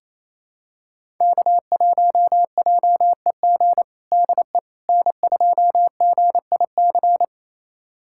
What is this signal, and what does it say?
Morse code, heard as a keyed tone, about 28 wpm, 700 Hz. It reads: K1JEG DE N2GIC
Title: USA Callsign DE Callsign 28wpm